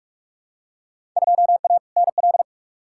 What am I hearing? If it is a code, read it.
Morse code, heard as a keyed tone, about 46 wpm, 700 Hz. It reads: 2A NL
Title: Field Day Exchanges Encore - 45wpm